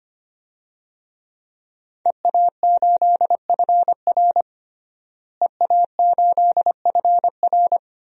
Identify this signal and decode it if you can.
Morse code, heard as a keyed tone, about 25 wpm, 700 Hz. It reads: EA8FR EA8FR
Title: DXCC-100 Call Signs - 5 Characters - Repeated 3x - 25wpm